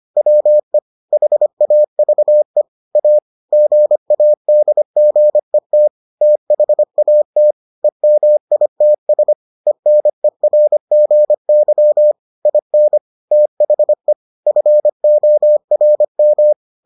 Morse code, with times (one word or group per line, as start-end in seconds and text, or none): 0.2-0.8 WE
1.1-2.6 HAVE
3.0-3.2 A
3.5-5.9 GADGET
6.2-7.5 THAT
7.8-9.3 EMITS
9.7-12.1 ENERGY
12.5-13.0 IN
13.3-14.1 THE
14.5-16.5 FORM